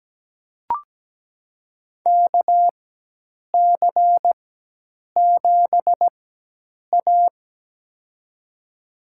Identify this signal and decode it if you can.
Morse code, heard as a keyed tone, about 17 wpm, spaced as for 8 wpm, 700 Hz. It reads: KC7A
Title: USA Call Signs - 4 Characters - 4x Character Spacing - 17wpm